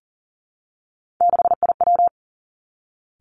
Morse code, heard as a keyed tone, about 40 wpm, 700 Hz. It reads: 6IW
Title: Three Challenging Letter-Number Combinations 40wpm